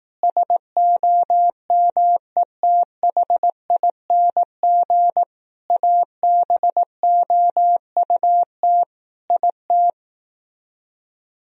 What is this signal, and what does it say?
Morse code, heard as a keyed tone, about 18 wpm, 700 Hz. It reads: SOMETHING ABOUT IT
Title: Speed-Racing - Sets of 5 Words 15wpm